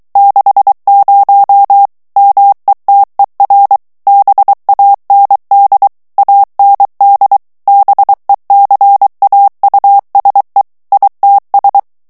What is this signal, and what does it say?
Morse code, heard as a keyed tone, about 23 wpm, 800 Hz. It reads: 60 METER BAND AND BECAUSE ITS
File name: ANGLAIS texte clair 3 vitesse 22 mots minute